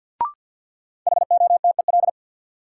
Morse code, heard as a keyed tone, about 49 wpm, 700 Hz. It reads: HOTEL
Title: Sentences from Top 1600 Words - New Word Drill Alt - 50wpm